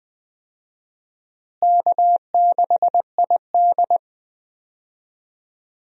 Morse code, heard as a keyed tone, about 20 wpm, 700 Hz. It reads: K6ID
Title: USA Call Signs - 4 Characters - Repeated 2x - 20wpm